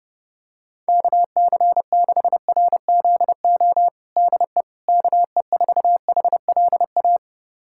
Morse code, heard as a keyed tone, about 30 wpm, 690 Hz. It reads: KC6RZO DE KE4HLA
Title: USA Callsign DE Callsign - 6 Characters - 30wpm